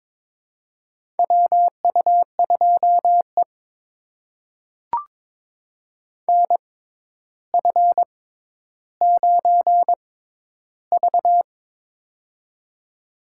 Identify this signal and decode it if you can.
Morse code, heard as a keyed tone, about 22 wpm, spaced as for 11 wpm, 700 Hz. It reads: WU2E NF9V
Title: USA Call Signs - 4 Characters - 6x Character Spacing - 22wpm